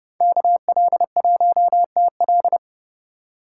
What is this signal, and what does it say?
Morse code, heard as a keyed tone, about 28 wpm, 700 Hz. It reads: KL1TL